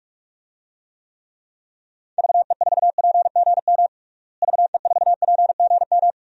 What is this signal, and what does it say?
Morse code, heard as a keyed tone, about 46 wpm, 700 Hz. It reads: VE4PGM VE4PGM
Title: DXCC-100 Call Signs - 6 Characters - Repeated 2x - 45wpm